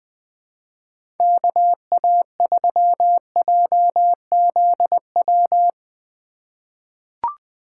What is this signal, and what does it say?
Morse code, heard as a keyed tone, about 20 wpm, 700 Hz. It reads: KA3JZW
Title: USA Call Signs - 6 Characters - 4x Character Spacing - 20wpm